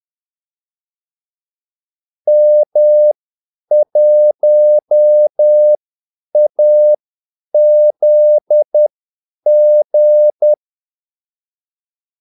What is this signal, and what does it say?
Morse code, heard as a keyed tone, about 10 wpm, spaced as for 8 wpm, 600 Hz. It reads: M1AZG
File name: Callsigns-10-8wpm